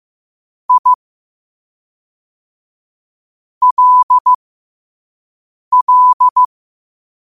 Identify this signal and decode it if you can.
Morse code, heard as a keyed tone, about 15 wpm, spaced as for 4 wpm, 1000 Hz. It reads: ILL